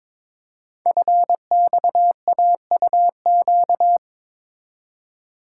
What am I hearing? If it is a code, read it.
Morse code, heard as a keyed tone, about 23 wpm, 700 Hz. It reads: FXAUQ
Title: Code Groups - 5 Letters - 22wpm